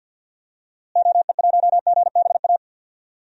Morse code, heard as a keyed tone, about 51 wpm, 700 Hz. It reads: OE1GBA
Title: DXCC-100 Call Signs - 6 Characters - Repeated 3x - 50wpm